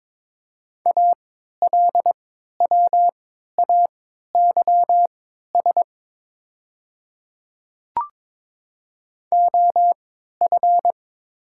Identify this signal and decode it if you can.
Morse code, heard as a keyed tone, about 22 wpm, spaced as for 13 wpm, 700 Hz. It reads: ALWAYS OF
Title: 3x Character Spacing - Top 300 Words - 22wpm